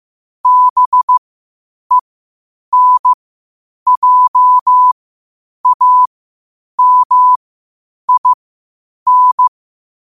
Morse code, heard as a keyed tone, about 15 wpm, spaced as for 8 wpm, 1000 Hz. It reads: BENJAMIN
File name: qotdfiles / 8wpm